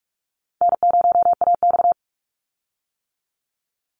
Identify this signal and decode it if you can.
Morse code, heard as a keyed tone, about 46 wpm, 700 Hz. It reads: N0AX
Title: Easy Call Signs - 45wpm